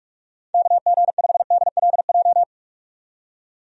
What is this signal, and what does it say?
Morse code, heard as a keyed tone, about 41 wpm, 700 Hz. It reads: KG5DLJ
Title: USA Call Signs - 6 Characters - 45wpm